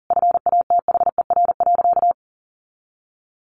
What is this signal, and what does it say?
Morse code, heard as a keyed tone, about 40 wpm, 700 Hz. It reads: FATHER.
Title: Sentences from Top 1500 Words 40wpm